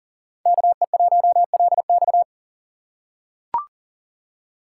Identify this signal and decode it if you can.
Morse code, heard as a keyed tone, about 40 wpm, 700 Hz. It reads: KE1LX
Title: USA Call Signs - 3x Character Spacing - 40wpm